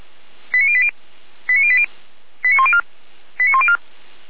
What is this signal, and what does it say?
Five-tone selective call (ZVEI), heard as quick runs of five tones, tones of about 80 ms. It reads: 89080 89080 89104 89104
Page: SDR/ZVEI